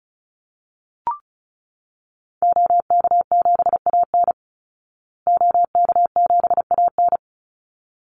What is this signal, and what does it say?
Morse code, heard as a keyed tone, about 34 wpm, 700 Hz. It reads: OK7AN OK7AN
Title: Call Signs Repeated 3x - 35wpm